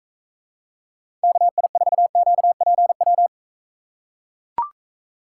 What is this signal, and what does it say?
Morse code, heard as a keyed tone, about 42 wpm, 700 Hz. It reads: KI4QPW